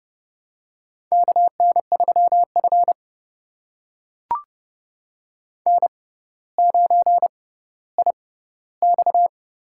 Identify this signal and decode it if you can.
Morse code, heard as a keyed tone, about 30 wpm, spaced as for 15 wpm, 700 Hz. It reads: KN3F N9IX
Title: USA Call Signs - 4 Characters - 6x Character Spacing - 30wpm